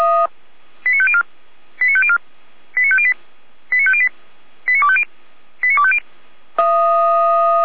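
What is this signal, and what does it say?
Five-tone selective call (ZVEI), heard as quick runs of five tones, tones of about 70 ms. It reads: 89504 89504 89508 89508 89250 89250
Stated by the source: ZVEI five-tone selective call 89229 received by SDR.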